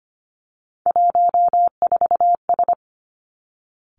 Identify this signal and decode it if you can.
Morse code, heard as a keyed tone, about 25 wpm, 700 Hz. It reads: 14S
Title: Three Challenging Letter-Number Combinations 25wpm